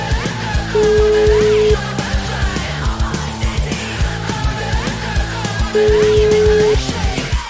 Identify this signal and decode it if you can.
Ringback tone (Europe, 425 Hz), with noise.
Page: Uvítací melodii uslyší volající, než přijmete hovor.